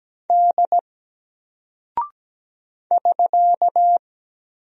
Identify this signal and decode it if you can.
Morse code, heard as a keyed tone, about 17 wpm, 700 Hz. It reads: D <SK>
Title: Single Character (Letter:Number:Punctuation:Prosign) 17wpm